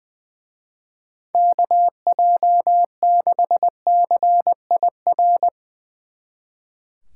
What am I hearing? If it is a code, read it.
Morse code, heard as a keyed tone, about 22 wpm, 700 Hz. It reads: KJ6CIR
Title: Speed Racing - USA 2x3 Call Signs 20wpm